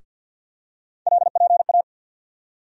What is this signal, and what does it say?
Morse code, heard as a keyed tone, about 50 wpm, 700 Hz. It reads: LPA
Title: USA Call Sign Postfixes - 3 Letters 50wpm